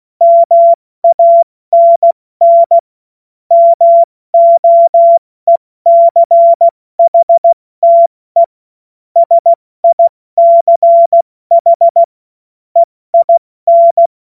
Morse code, heard as a keyed tone, about 16 wpm, 680 Hz. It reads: MANN MOECHTE SICH EIN
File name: Morsekurs / Witze / Witz 07. Februar 2023 70 F80